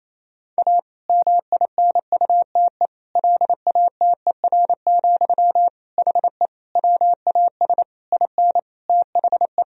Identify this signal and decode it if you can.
Morse code, heard as a keyed tone, about 28 wpm, 700 Hz. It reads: A MINUTE LATER, HE WAS IN THE
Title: Sentences from Top 1000 Words - Review Part 4 - 28wpm